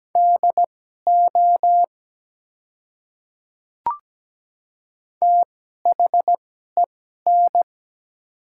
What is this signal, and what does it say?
Morse code, heard as a keyed tone, about 17 wpm, spaced as for 12 wpm, 700 Hz. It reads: DO THEN
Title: 2x Character Spacing - Top 200 Words - 17wpm